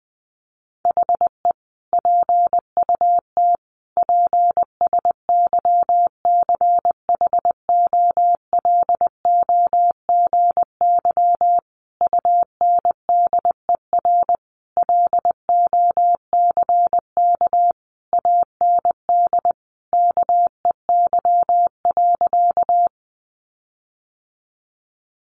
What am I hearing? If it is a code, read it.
Morse code, heard as a keyed tone, about 20 wpm, 700 Hz. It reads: HE PUT PSYCHOLOGY UNDER LOCK AND KEY.
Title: Sentences from Top 1400 Words 20wpm